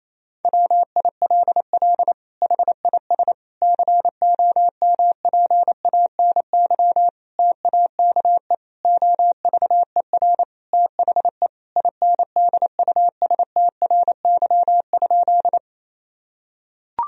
Morse code, heard as a keyed tone, about 28 wpm, 700 Hz. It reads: WILL HIS COMPANY TAKE OVER THE INDUSTRY?
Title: Sentences from Top 1000 Words - Review Part 4 - 28wpm